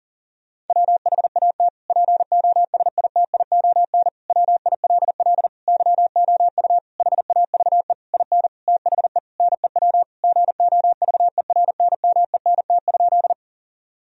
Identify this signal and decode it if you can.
Morse code, heard as a keyed tone, about 40 wpm, 700 Hz. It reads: WHAT POSITION WILL YOU HAVE IN THE NEW GOVERNMENT?
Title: Sentences from Top 1000 Words - Review Part 3 - 40wpm